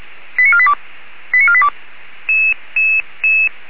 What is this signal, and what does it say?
Five-tone selective call (ZVEI), heard as quick runs of five tones, tones of about 70 ms. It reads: 89482 89482